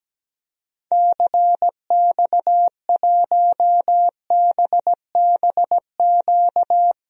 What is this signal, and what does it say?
Morse code, heard as a keyed tone, about 17 wpm, 700 Hz. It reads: CX1BBQ
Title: DXCC-100 Call Signs - 6 Characters - 17wpm